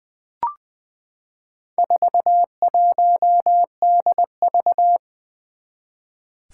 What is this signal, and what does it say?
Morse code, heard as a keyed tone, about 20 wpm, 700 Hz. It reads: 41DV
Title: Four Challenging Letter-Number Combinations 20wpm